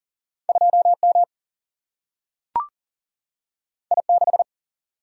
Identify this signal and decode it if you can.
Morse code, heard as a keyed tone, about 40 wpm, 700 Hz. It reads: 2 M I 6
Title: Single-Letter Single-Number 40wpm